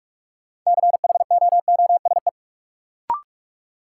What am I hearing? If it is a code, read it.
Morse code, heard as a keyed tone, about 44 wpm, 700 Hz. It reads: CHOOSE
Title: Sentences from Top 700 Words - New Word Drill - 45wpm